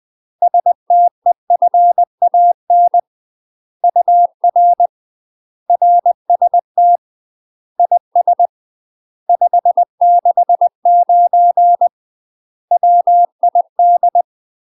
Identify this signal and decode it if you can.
Morse code, heard as a keyed tone, about 20 wpm, 700 Hz. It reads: STEFAN UR RST IS 569 WID
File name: UK MASTER QSOs TRAINING / 20WPM / UK MASTER QSOs 0052